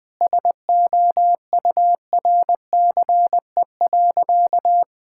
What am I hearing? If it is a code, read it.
Morse code, heard as a keyed tone, about 21 wpm, 700 Hz. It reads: SOURCE.
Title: Sentences from Top 1000 Words - Review Part 5 - 20wpm